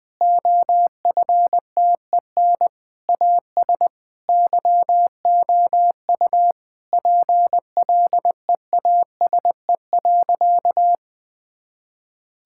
Morse code, heard as a keyed tone, about 20 wpm, 700 Hz. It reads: OFTEN AS YOU PLEASE.
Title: Sentences from Top 1600 Words 20wpm